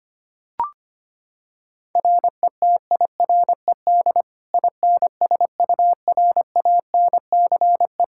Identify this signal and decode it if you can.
Morse code, heard as a keyed tone, about 25 wpm, 700 Hz. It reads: RETIRED INSURANCE
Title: Practice QSO - Exchange 3 - More Info - 25wpm